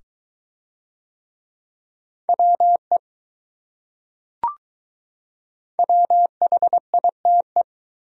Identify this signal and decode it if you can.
Morse code, heard as a keyed tone, about 21 wpm, 700 Hz. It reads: WE WHITE